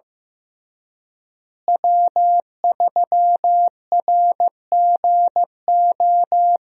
Morse code, heard as a keyed tone, about 15 wpm, 700 Hz. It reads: W3RGO
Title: USA Call Signs - 5 Characters - Repeated 3x - 15wpm